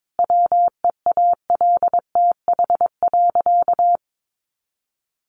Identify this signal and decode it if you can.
Morse code, heard as a keyed tone, about 22 wpm, 700 Hz. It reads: WEALTH.
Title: Sentences from Top 1300 Words 22wpm